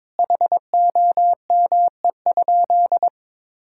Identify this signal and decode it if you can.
Morse code, heard as a keyed tone, about 22 wpm, 700 Hz. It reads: HOME?
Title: Sentences from Top 1600 Words 22wpm